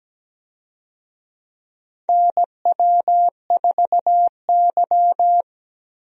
Morse code, heard as a keyed tone, about 17 wpm, 700 Hz. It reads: NW4Y
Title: USA Call Signs - 4 Characters - Repeated 2x - 17wpm